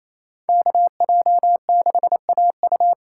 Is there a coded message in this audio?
Yes: Morse code from 0.5 to 2.9 s, heard as a keyed tone, about 36 wpm, 700 Hz. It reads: KJ6AU